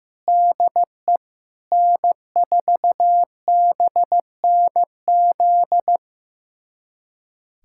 Morse code, heard as a keyed tone, about 15 wpm, 700 Hz. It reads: DE N4BNZ
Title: USA Callsign DE Callsign - 5 Characters - 15wpm